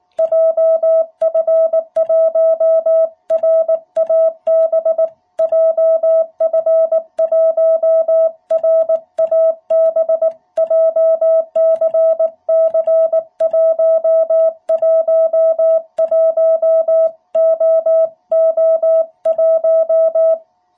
Morse code, heard as a keyed tone, about 18 wpm, 660 Hz. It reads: JF1RAB JF1RAB JCC111OO1
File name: jh8cbh / 2021cw